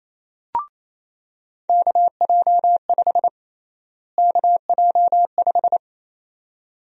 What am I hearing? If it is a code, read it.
Morse code, heard as a keyed tone, about 28 wpm, 700 Hz. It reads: KJ5 KJ5
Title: USA Call Sign Prefixes - Repeated 2x - 28wpm